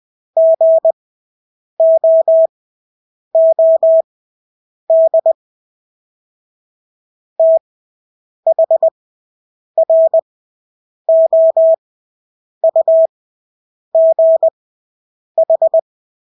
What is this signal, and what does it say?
Morse code, heard as a keyed tone, about 20 wpm, spaced as for 8 wpm, 650 Hz. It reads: GOOD THROUGH
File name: Engelske ord 20/8 wpm